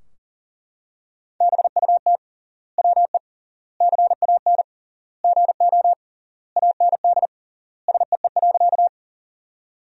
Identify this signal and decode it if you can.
Morse code, heard as a keyed tone, about 40 wpm, 700 Hz. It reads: BUT WE CAN GO AND SEE.
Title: 3x Word-Spacing - Sentences from Top 100 Words - 40wpm